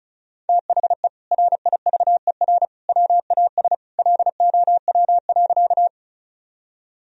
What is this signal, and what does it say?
Morse code, heard as a keyed tone, about 36 wpm, 700 Hz. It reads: THE RIVER WAS LOW.
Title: Sentences from Top 1000 Words - Review Part 1 - 35wpm